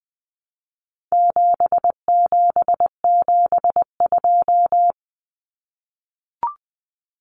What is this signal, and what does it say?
Morse code, heard as a keyed tone, about 20 wpm, 700 Hz. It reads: 7772
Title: Numbers - 4-Digit 20wpm